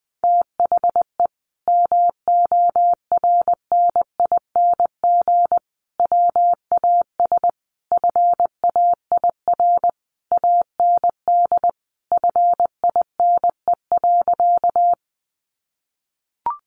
Morse code, heard as a keyed tone, about 20 wpm, 700 Hz. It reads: THE MORNING WAS FAIR AND FINE.
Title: Sentences from Top 1300 Words 20wpm